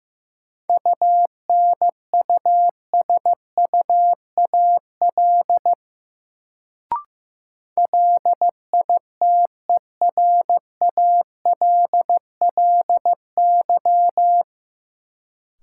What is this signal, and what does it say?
Morse code, heard as a keyed tone, about 15 wpm, 700 Hz. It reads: UNUSUAL LITERALLY
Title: Sentences from Top 1400 Words - New Word Drill - 15wpm